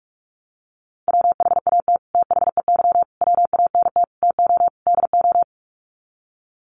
Morse code, heard as a keyed tone, about 44 wpm, 700 Hz. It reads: WHAT THEY WANT TO DO
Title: Sets of 5 Words Encore2 45wpm